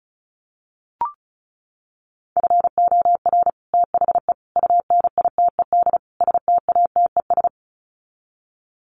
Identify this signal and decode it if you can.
Morse code, heard as a keyed tone, about 34 wpm, 700 Hz. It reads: FOR THE UNITED STATES
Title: Sets of 4 Words 35wpm